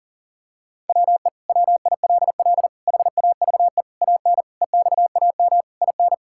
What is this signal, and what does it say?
Morse code, heard as a keyed tone, about 40 wpm, 700 Hz. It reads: WE WILL HAVE AN EXAM IN
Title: Sentences from Top 1000 Words - Review Part 5 - 40wpm